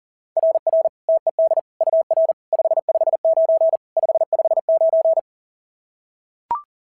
Morse code, heard as a keyed tone, about 40 wpm, 690 Hz. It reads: RR TED UR 559 559
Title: SOTA QSO Practice 40wpm